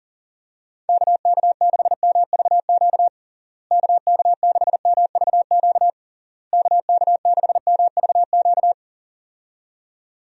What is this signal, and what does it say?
Morse code, heard as a keyed tone, about 40 wpm, 700 Hz. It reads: KK6MVQ KK6MVQ KK6MVQ
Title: USA Call Signs - 6 Characters - Repeated 3x - 40wpm